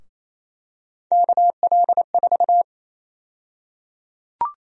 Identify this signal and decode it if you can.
Morse code, heard as a keyed tone, about 28 wpm, 700 Hz. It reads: KL4
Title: USA Call Sign Prefixes - 28wpm